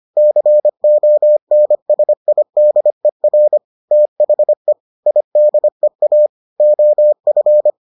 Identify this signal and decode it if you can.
Morse code, heard as a keyed tone, about 25 wpm, 600 Hz. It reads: CONSIDER THE IDEA OF